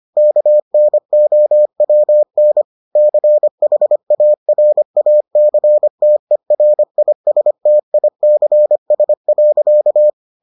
Morse code, heard as a keyed tone, about 25 wpm, 600 Hz. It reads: KNOWN CHARACTERISTICS.